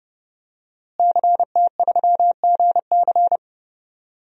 Morse code, heard as a keyed tone, about 30 wpm, 700 Hz. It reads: CT3GC
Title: DXCC-100 Call Signs - 5 Characters - Repeated 3x - 30wpm